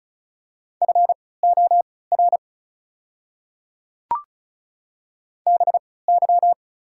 Morse code, heard as a keyed tone, about 36 wpm, spaced as for 20 wpm, 690 Hz. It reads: FOR BY
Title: 3x Character Spacing - Top 100 Words - 35wpm